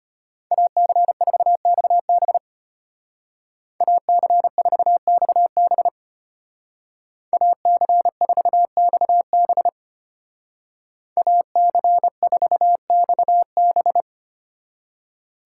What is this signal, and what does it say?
Morse code, heard as a keyed tone, about 30 wpm, 700 Hz. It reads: AC4XB AC4XB AC4XB AC4XB
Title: Speed Racing - USA Call Signs 25wpm